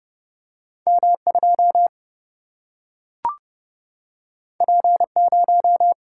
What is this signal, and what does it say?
Morse code, heard as a keyed tone, about 30 wpm, 700 Hz. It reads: M2 P0
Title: Single-Letter Single-Number 30wpm